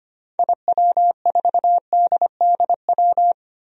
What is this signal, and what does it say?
Morse code, heard as a keyed tone, about 25 wpm, 700 Hz. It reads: IW4DDW
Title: DXCC-100 Call Signs - 6 Characters - Repeated 2x - 25wpm